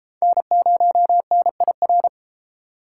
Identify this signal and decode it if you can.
Morse code, heard as a keyed tone, about 29 wpm, 700 Hz. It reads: N0NIR